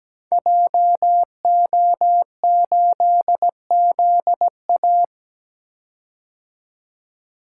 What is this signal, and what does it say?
Morse code, heard as a keyed tone, about 17 wpm, 700 Hz. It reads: JO8ZA
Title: Code Groups - 5 Letter:Numbers - 17wpm